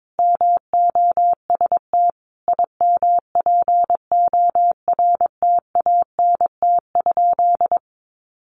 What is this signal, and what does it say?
Morse code, heard as a keyed tone, about 22 wpm, 700 Hz. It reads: MOST IMPORTANT?
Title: Sentences from Top 1300 Words 22wpm